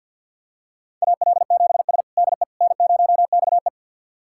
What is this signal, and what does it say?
Morse code, heard as a keyed tone, about 53 wpm, 700 Hz. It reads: AL7S DE N0XE
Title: USA Callsign DE Callsign - 4 Characters - 50wpm